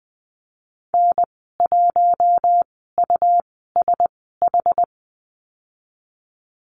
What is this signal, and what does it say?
Morse code, heard as a keyed tone, about 20 wpm, spaced as for 15 wpm, 700 Hz. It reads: N1USH
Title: 128 Lesson 21g - Number 6 - Call Signs 20wpm